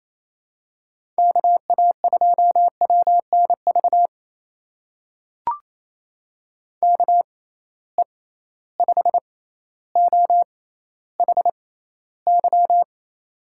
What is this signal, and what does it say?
Morse code, heard as a keyed tone, about 27 wpm, spaced as for 14 wpm, 700 Hz. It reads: KA2WNV KE5OHY
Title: USA Call Signs - 6 Characters - 6x Character Spacing - 28wpm